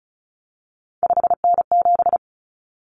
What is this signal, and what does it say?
Morse code, heard as a keyed tone, about 34 wpm, 700 Hz. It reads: 5N7
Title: Common Cut Numbers - 3-Digit 35wpm